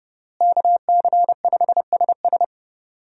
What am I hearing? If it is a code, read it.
Morse code, heard as a keyed tone, about 32 wpm, 700 Hz. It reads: KC5SS
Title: USA Call Signs - 5 Characters - Repeated 2x - 30wpm